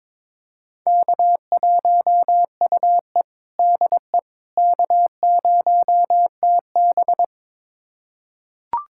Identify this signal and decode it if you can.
Morse code, heard as a keyed tone, about 22 wpm, 700 Hz. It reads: K1UE DE K0TB
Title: USA Callsign DE Callsign - 4 Characters - 22wpm